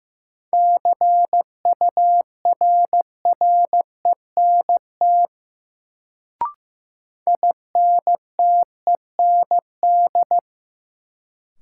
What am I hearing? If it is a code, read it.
Morse code, heard as a keyed tone, about 15 wpm, 700 Hz. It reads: CURRENT INTEND
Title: Sentences from Top 1600 Words - New Word Drill Alt - 15wpm